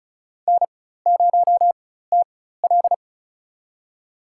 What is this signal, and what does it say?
Morse code, heard as a keyed tone, about 36 wpm, spaced as for 16 wpm, 700 Hz. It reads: N0TL
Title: USA Call Signs - 4 Characters - 4x Character Spacing - 35wpm